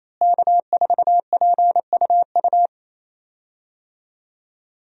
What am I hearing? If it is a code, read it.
Morse code, heard as a keyed tone, about 28 wpm, 700 Hz. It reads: K4PUU
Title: USA Call Signs - 5 Characters - Repeated 3x - 28wpm